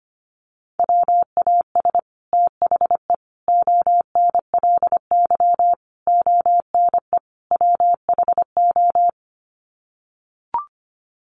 Morse code, heard as a keyed tone, about 25 wpm, 700 Hz. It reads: WAS THE ONLY ONE WHO
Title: Sets of 5 Words Encore2 25wpm